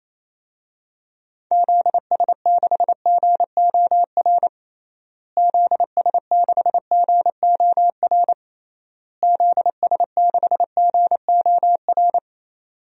Morse code, heard as a keyed tone, about 28 wpm, 700 Hz. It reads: ZS6GOR ZS6GOR ZS6GOR
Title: DXCC-100 Call Signs - 6 Characters - Repeated 3x - 28wpm